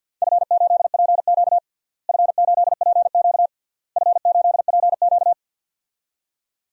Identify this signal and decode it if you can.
Morse code, heard as a keyed tone, about 52 wpm, 700 Hz. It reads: F8PQ F8PQ F8PQ
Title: DXCC-100 Call Signs - 4 Characters - Repeated 3x - 50wpm